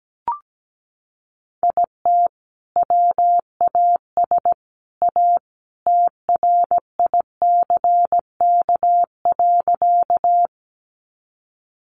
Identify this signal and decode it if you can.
Morse code, heard as a keyed tone, about 17 wpm, 700 Hz. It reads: IT WAS A TRICK.
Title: Sentences from Top 1500 Words 17wpm